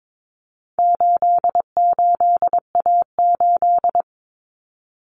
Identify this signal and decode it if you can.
Morse code, heard as a keyed tone, about 22 wpm, 700 Hz. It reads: 88A8
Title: Common Cut Numbers - 4-Digit 22wpm